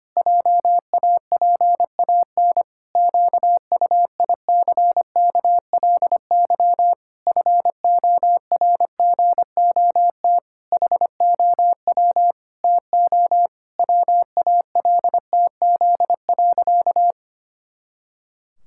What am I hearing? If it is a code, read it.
Morse code, heard as a keyed tone, about 25 wpm, 700 Hz. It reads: JAPAN QUICKLY FORGOT HOW TO WALTZ.
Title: Pangram Challenge v2 - 25wpm